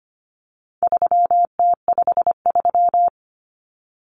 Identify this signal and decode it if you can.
Morse code, heard as a keyed tone, about 25 wpm, 700 Hz. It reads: 3T53
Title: Common Cut Numbers - 4-Digit 25wpm